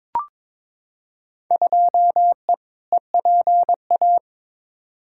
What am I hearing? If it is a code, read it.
Morse code, heard as a keyed tone, about 22 wpm, 700 Hz. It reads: 2E EPA
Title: Field Day Exchanges Encore - 22wpm